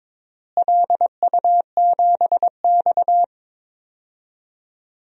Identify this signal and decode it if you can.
Morse code, heard as a keyed tone, about 21 wpm, 700 Hz. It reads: LU7X
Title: DXCC-100 Call Signs - 4 Characters - Repeated 3x - 22wpm